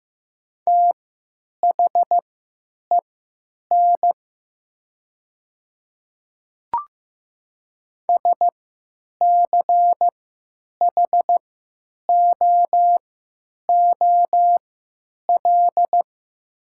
Morse code, heard as a keyed tone, about 15 wpm, spaced as for 9 wpm, 700 Hz. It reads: THEN SCHOOL
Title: 3x Character Spacing - Top 300 Words - 15wpm